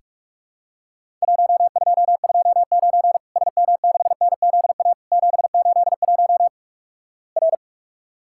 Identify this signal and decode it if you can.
Morse code, heard as a keyed tone, about 47 wpm, 700 Hz. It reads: 1229 SM6NZA 781 R
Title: Worked All Europe DX Contest - QTC Exchange - 45wpm